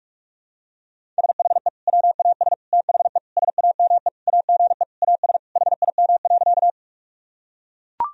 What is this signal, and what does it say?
Morse code, heard as a keyed tone, about 41 wpm, 700 Hz. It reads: SHE WAS THE SAME AGE AS HIM.
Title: Sentences from Top 1000 Words - Review Part 3 - 45wpm